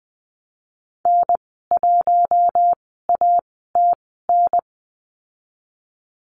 Morse code, heard as a keyed tone, about 20 wpm, spaced as for 14 wpm, 700 Hz. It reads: N1ATN
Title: 055 Lesson 10g - Letter r - Call Signs 20wpm